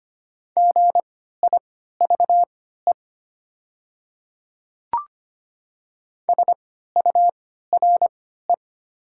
Morse code, heard as a keyed tone, about 25 wpm, spaced as for 14 wpm, 700 Hz. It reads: GIVE SURE
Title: 3x Character Spacing - Top 300 Words - 25wpm